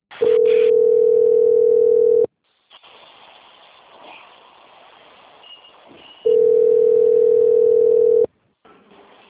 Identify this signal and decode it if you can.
Ringback tone (North America), 440 + 480 Hz.
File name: out- - - - - .15839